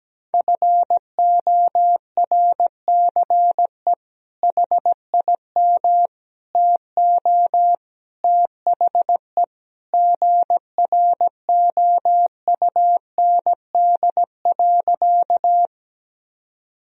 Morse code, heard as a keyed tone, about 17 wpm, 700 Hz. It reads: FORCE HIM TO THE GROUND.
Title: Sentences from Top 1000 Words - Review Part 3 - 17wpm